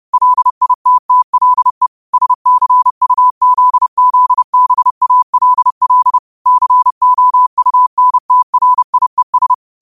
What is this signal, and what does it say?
Morse code, heard as a keyed tone, about 30 wpm, 1000 Hz. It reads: LITTLE SCUZZBALL COUNTRIES